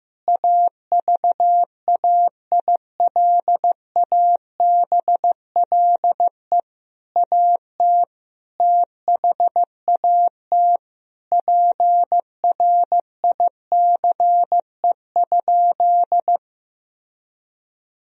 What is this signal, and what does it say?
Morse code, heard as a keyed tone, about 15 wpm, 700 Hz. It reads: AVAILABLE AT THAT PRICE?
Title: Sentences from Top 1000 Words - Review Part 3 - 15wpm